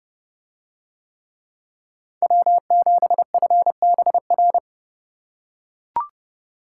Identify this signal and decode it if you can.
Morse code, heard as a keyed tone, about 28 wpm, 700 Hz. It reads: W7FBR